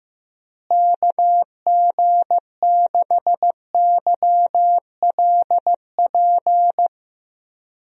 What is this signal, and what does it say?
Morse code, heard as a keyed tone, about 15 wpm, 700 Hz. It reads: KG6YLP
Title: USA Call Signs - 6 Characters - Repeated 3x - 15wpm